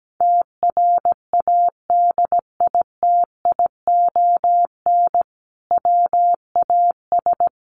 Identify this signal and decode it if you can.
Morse code, heard as a keyed tone, about 17 wpm, 700 Hz. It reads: TRADITION WAS
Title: Sentences from Top 1500 Words 17wpm